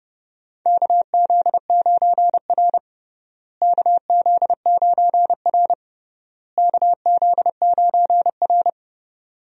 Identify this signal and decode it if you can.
Morse code, heard as a keyed tone, about 32 wpm, 700 Hz. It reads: KZ9R KZ9R KZ9R
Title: USA Call Signs - 4 Characters - Repeated 3x - 30wpm